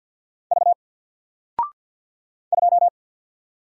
Morse code, heard as a keyed tone, about 48 wpm, 700 Hz. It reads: V 2
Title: Single Letter-Number Alt 50wpm